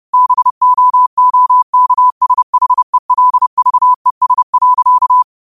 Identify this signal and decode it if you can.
Morse code, heard as a keyed tone, about 30 wpm, 1000 Hz. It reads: BOOKSHELVES.